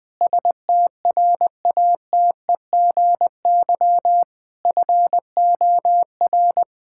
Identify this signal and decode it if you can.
Morse code, heard as a keyed tone, about 21 wpm, 700 Hz. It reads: STRATEGY FOR
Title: Sentences from Top 1000 Words - Review Part 5 - 20wpm